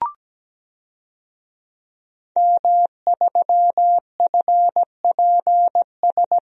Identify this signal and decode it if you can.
Morse code, heard as a keyed tone, about 17 wpm, 700 Hz. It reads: M3FPS
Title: DXCC-100 Call Signs - 5 Characters - 17wpm